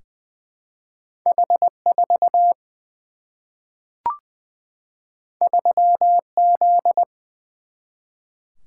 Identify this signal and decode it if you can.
Morse code, heard as a keyed tone, about 20 wpm, 700 Hz. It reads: H4 3Z
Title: Single-Letter Single-Number 20wpm